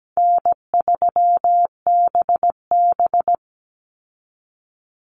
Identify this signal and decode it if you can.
Morse code, heard as a keyed tone, about 17 wpm, 700 Hz. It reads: N3BB
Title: Call Signs Repeated 3x - 17wpm